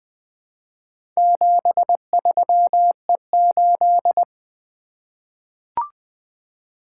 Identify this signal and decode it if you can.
Morse code, heard as a keyed tone, about 20 wpm, 700 Hz. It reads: 73E8
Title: Four Letter-Number Combinations - 20wpm